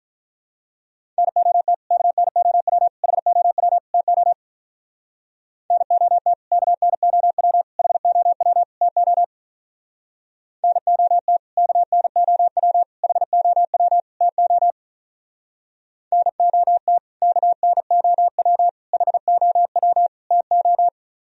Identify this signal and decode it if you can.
Morse code, heard as a keyed tone, about 42 wpm, 700 Hz. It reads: NOT KNOW HOW TO NOT KNOW HOW TO NOT KNOW HOW TO NOT KNOW HOW TO
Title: Speed-Racing - Sets of 4 Words 35wpm